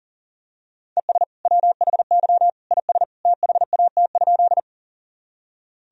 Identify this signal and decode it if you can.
Morse code, heard as a keyed tone, about 37 wpm, 700 Hz. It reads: ES WHY IS THAT?
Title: Sentences from Top 200 Words and Abbreviations 40wpm